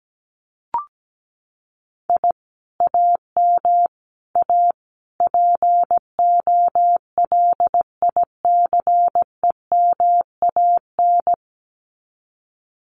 Morse code, heard as a keyed tone, about 17 wpm, 700 Hz. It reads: I AM A POLICEMAN
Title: Common QSO Phrases Encore 17wpm